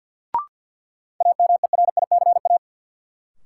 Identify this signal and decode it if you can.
Morse code, heard as a keyed tone, about 50 wpm, 700 Hz. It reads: AMERICA
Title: Sets of 5 Words - New Word Drill - 50wpm